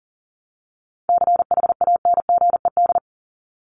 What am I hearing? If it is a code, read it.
Morse code, heard as a keyed tone, about 40 wpm, 700 Hz. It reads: CHANGED
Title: ED Suffix - Top 100 Words with Base Words 40wpm